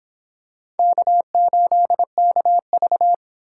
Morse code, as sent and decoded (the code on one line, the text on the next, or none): -.- ---.. -.- ...-
K8KV